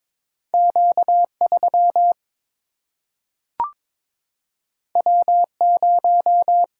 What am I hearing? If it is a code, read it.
Morse code, heard as a keyed tone, about 22 wpm, 700 Hz. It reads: Q 3 W 0
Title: Single-Letter Single-Number 22wpm